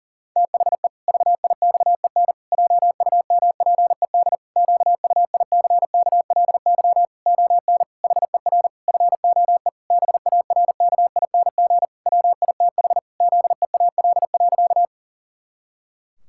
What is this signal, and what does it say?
Morse code, heard as a keyed tone, about 40 wpm, 700 Hz. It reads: THE VIXEN JUMPED QUICKLY ON HER FOE BARKING WITH ZEAL.
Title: Pangram Challenge v2 - 40wpm